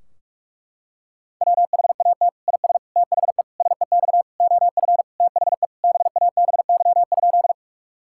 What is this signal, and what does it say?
Morse code, heard as a keyed tone, about 41 wpm, 700 Hz. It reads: WHAT IS THE SEX OF THE BABY?
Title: Sentences from Top 1600 Words 45wpm